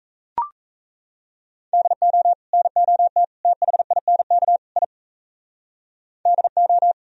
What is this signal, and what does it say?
Morse code, heard as a keyed tone, about 37 wpm, 700 Hz. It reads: DO NOT THINK I DO